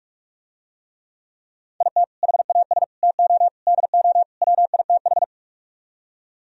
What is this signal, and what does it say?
Morse code, heard as a keyed tone, about 47 wpm, 700 Hz. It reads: IT HAS TO DO WITH